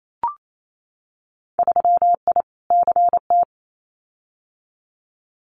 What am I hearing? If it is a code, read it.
Morse code, heard as a keyed tone, about 28 wpm, 700 Hz. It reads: 3I CT
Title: Winter Field Day Exchanges 28wpm